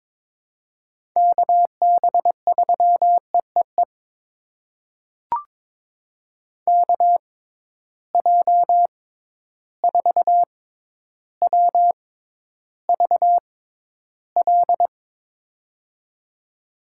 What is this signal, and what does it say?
Morse code, heard as a keyed tone, about 21 wpm, spaced as for 11 wpm, 700 Hz. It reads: KB3EEE KJ4WVL
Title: USA Call Signs - 6x Character Spacing - 22wpm